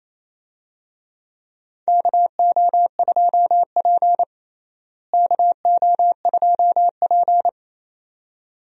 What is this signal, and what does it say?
Morse code, heard as a keyed tone, about 28 wpm, 700 Hz. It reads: KO2P KO2P
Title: USA Call Signs - 4 Characters - Repeated 2x - 28wpm